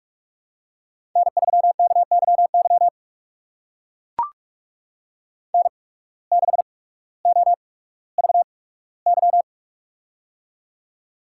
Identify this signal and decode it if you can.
Morse code, heard as a keyed tone, about 44 wpm, spaced as for 19 wpm, 700 Hz. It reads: N3KYY N6OVY
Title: USA Call Signs - 5 Characters - 8x Character Spacing - 45wpm